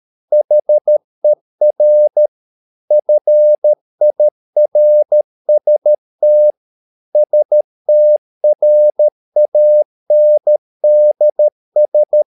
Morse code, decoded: HER FIRST STRANDS